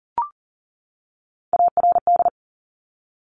Morse code, heard as a keyed tone, about 40 wpm, 700 Hz. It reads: ARD
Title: Top Three Letter Combinations in Words 40wpm